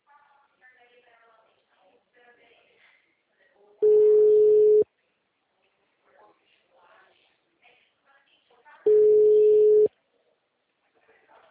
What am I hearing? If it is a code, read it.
Ringback tone (Europe, 425 Hz), with speech.